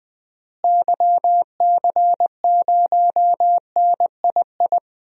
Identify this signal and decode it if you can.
Morse code, heard as a keyed tone, about 20 wpm, 700 Hz. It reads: YC0NII
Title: DXCC-100 Call Signs - 6 Characters - Repeated 2x - 20wpm